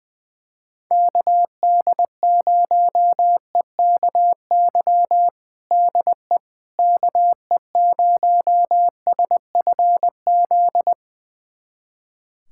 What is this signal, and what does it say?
Morse code, heard as a keyed tone, about 21 wpm, 700 Hz. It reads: KD0EKY DE KE0SFZ
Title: USA Callsign DE Callsign - 6 Characters - 20wpm